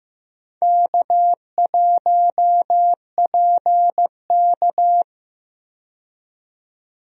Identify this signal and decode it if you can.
Morse code, heard as a keyed tone, about 15 wpm, 700 Hz. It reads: K1PK
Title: USA Call Signs - 4 Characters - 15wpm